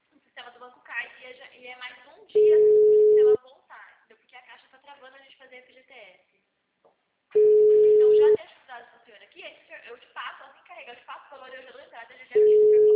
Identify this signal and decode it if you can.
Ringback tone (Europe, 425 Hz), with speech.
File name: out- -2013- - - .16328